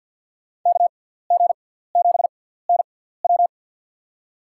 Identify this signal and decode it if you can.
Morse code, heard as a keyed tone, about 51 wpm, spaced as for 17 wpm, 700 Hz. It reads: KG7NW
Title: USA Call Signs - 5 Characters - 6x Character Spacing - 50wpm